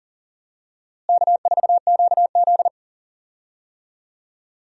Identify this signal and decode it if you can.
Morse code, heard as a keyed tone, about 40 wpm, 700 Hz. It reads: K4QZ
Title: USA Call Signs - 4 Characters - Repeated 2x - 40wpm